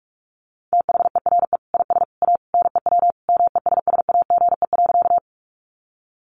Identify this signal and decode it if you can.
Morse code, heard as a keyed tone, about 44 wpm, 700 Hz. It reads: THERE IS A NEW MESSAGE.
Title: Sentences from Top 1100 Words 45wpm